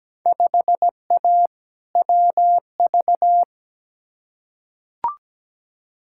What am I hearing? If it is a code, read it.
Morse code, heard as a keyed tone, about 17 wpm, 700 Hz. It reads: 5A WV
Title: Field Day Exchanges - 17wpm